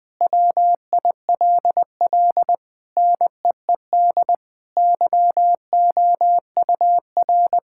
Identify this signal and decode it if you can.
Morse code, heard as a keyed tone, about 21 wpm, 700 Hz. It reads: WILL NEED YOUR
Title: Sentences from Top 1600 Words 20wpm